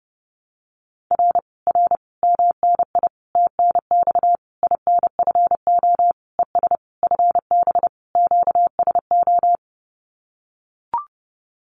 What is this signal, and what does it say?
Morse code, heard as a keyed tone, about 30 wpm, 700 Hz. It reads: R R MNI TNX INFO ES FB QSO
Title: Common QSO Phrases Encore 30wpm